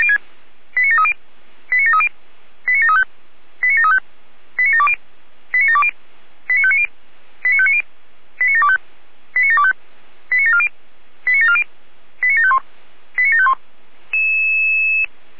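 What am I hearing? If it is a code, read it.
Five-tone selective call (ZVEI), heard as quick runs of five tones, tones of about 70 ms. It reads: 89730 89730 89735 89735 89720 89720 89590 89590 89725 89725 89740 89740 89751 89751
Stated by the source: ZVEI five-tone selective call 89705, received by SDR.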